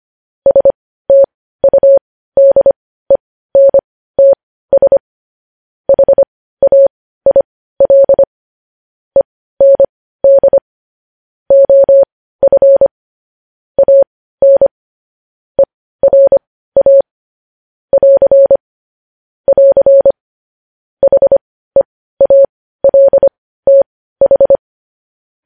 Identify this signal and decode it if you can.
Morse code, heard as a keyed tone, about 25 wpm, spaced as for 15 wpm, 550 Hz. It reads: STUDENTS HAIL END OF AN ERA <AR> <AR> HEALTH
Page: News Headlines in Morse Code at 15 WPM